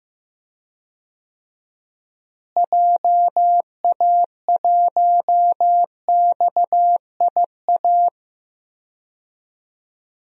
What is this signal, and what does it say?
Morse code, heard as a keyed tone, about 15 wpm, 700 Hz. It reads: JA1XIA
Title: DXCC-100 Call Signs - 6 Characters - Repeated 2x - 15wpm